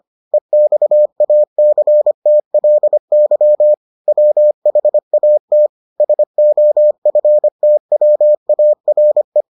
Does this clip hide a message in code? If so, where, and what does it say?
Yes: Morse code from 0.3 to 9.4 s, heard as a keyed tone, about 25 wpm, 600 Hz. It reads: EXACTLY WHAT SOFTWARE